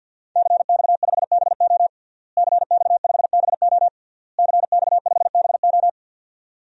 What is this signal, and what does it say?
Morse code, heard as a keyed tone, about 51 wpm, 700 Hz. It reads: CX5BO CX5BO CX5BO
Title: DXCC-100 Call Signs - 5 Characters - Repeated 3x - 50wpm